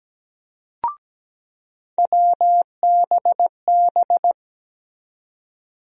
Morse code, heard as a keyed tone, about 17 wpm, 700 Hz. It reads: WBB
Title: USA Call Sign Postfixes - 3 Letters 17wpm